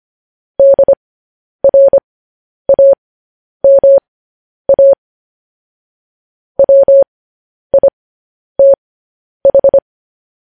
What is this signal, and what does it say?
Morse code, heard as a keyed tone, about 25 wpm, spaced as for 10 wpm, 550 Hz. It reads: DRAMA WITH
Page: news-headlines-in-morse-code-at-10-wpm